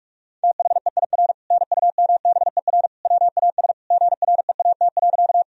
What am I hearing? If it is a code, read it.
Morse code, heard as a keyed tone, about 46 wpm, 700 Hz. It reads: THEIR NUMBER WAS GREAT.
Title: Sentences from Top 1000 Words - Review Part 1 - 45wpm